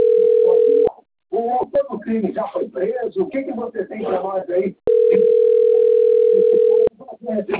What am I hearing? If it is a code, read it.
Ringback tone (North America), 440 + 480 Hz, with speech.
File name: rg-1600-unknown- - - .23686